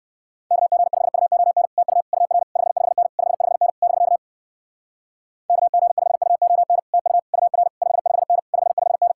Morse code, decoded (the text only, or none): KG5UON TU UR 55N 55N <BK> KG5UON TU UR 55N 55N